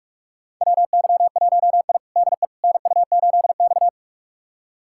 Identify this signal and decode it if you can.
Morse code, heard as a keyed tone, about 46 wpm, 690 Hz. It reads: WY1I DE NU8X
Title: USA Callsign DE Callsign - 4 Characters - 45wpm